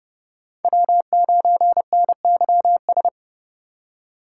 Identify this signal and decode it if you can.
Morse code, heard as a keyed tone, about 34 wpm, 700 Hz. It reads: W9NYS
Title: Speed Racing - USA 1x3 Call Signs 30wpm